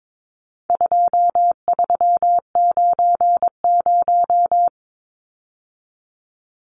Morse code, heard as a keyed tone, about 22 wpm, 700 Hz. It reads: 2390
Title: Numbers - 4-Digit 22wpm